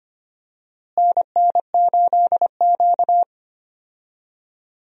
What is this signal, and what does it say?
Morse code, heard as a keyed tone, about 25 wpm, 700 Hz. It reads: NN8Q
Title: USA Call Signs - 4 Characters - Repeated 2x - 25wpm